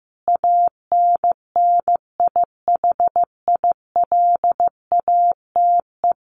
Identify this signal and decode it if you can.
Morse code, heard as a keyed tone, about 15 wpm, 700 Hz. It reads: ANNIHILATE
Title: A Sound Of Thunder 15wpm